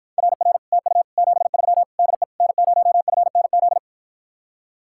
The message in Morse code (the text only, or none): RR TU 73 DE N0FNZ